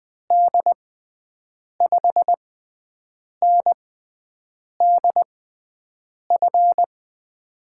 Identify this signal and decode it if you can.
Morse code, heard as a keyed tone, about 20 wpm, spaced as for 7 wpm, 700 Hz. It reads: D5NDF
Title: USA Call Signs - 6 Characters - 6x Character Spacing - 20wpm